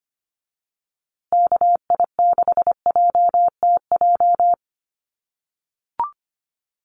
Morse code, heard as a keyed tone, about 25 wpm, 700 Hz. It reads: KI6JTJ
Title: USA Call Signs Repeated 2x 25wpm